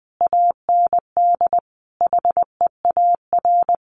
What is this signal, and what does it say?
Morse code, heard as a keyed tone, about 20 wpm, 700 Hz. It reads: AND HEAR
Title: Sentences from Top 1100 Words 20wpm